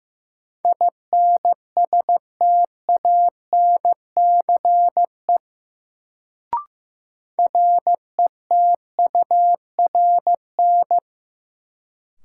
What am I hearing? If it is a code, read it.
Morse code, heard as a keyed tone, about 15 wpm, 700 Hz. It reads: INSTANCE RETURN
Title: Sentences from Top 800 Words - New Word Drill - 15wpm